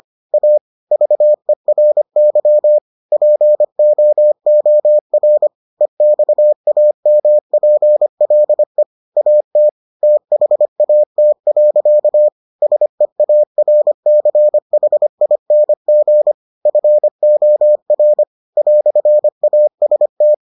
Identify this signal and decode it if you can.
Morse code, heard as a keyed tone, about 25 wpm, 600 Hz. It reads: A VERY POOR EXAMPLE AT THAT. SEARCHING FOR "AST